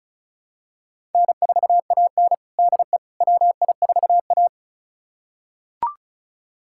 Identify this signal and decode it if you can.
Morse code, heard as a keyed tone, about 34 wpm, 700 Hz. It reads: N4AN DE WI4A
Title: USA Callsign DE Callsign - 4 Characters - 35wpm